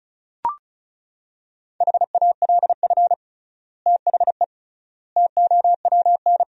Morse code, heard as a keyed tone, about 36 wpm, 700 Hz. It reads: HALF THE TOWN
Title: 3x Word-Spacing - Sentences from Top 500 Words - 35wpm